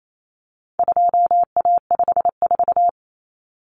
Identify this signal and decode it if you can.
Morse code, heard as a keyed tone, about 28 wpm, 700 Hz. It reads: 2A54
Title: Common Cut Numbers - 4-Digit 28wpm